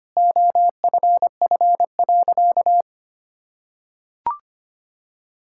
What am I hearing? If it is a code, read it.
Morse code, heard as a keyed tone, about 25 wpm, 700 Hz. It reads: OFF.
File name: practice / 25wpm / Sentences with 2-Word Phrases - 25wpm